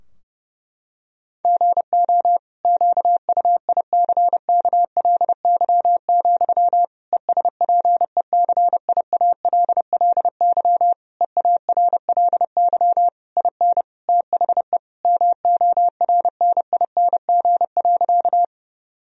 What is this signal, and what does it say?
Morse code, heard as a keyed tone, about 32 wpm, 700 Hz. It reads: GO QUICKLY, ESPECIALLY EARLY IN THE MORNING.
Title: Sentences from Top 1000 Words - Review Part 4 - 30wpm